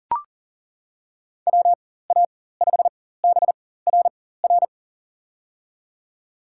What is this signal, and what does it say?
Morse code, heard as a keyed tone, about 40 wpm, spaced as for 19 wpm, 700 Hz. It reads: WA5BRR
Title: USA Call Signs - 6 Characters - 4x Character Spacing - 40wpm